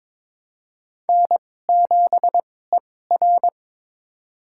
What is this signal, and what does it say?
Morse code, heard as a keyed tone, about 22 wpm, spaced as for 16 wpm, 700 Hz. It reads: N7ER
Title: USA Call Signs - 4 Characters - 2x Character Spacing - 22wpm